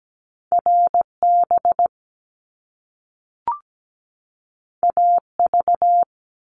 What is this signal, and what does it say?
Morse code, heard as a keyed tone, about 17 wpm, 700 Hz. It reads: RB AV
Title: Top Two Letter Combinations in Words 17wpm